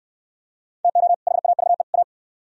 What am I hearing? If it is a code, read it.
Morse code, heard as a keyed tone, about 63 wpm, 700 Hz. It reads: TO HAVE A